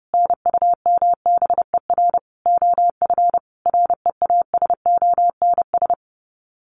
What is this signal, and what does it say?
Morse code, heard as a keyed tone, about 30 wpm, 700 Hz. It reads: NUMBER OF REASONS
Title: Sets of 5 Words Encore2 30wpm